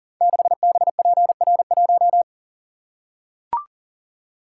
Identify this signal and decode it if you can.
Morse code, heard as a keyed tone, about 39 wpm, 690 Hz. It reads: 6BPR1
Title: Code Groups - 5 Letter:Numbers - 40wpm